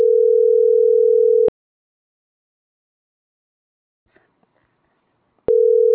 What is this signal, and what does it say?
Ringback tone (North America), 440 + 480 Hz.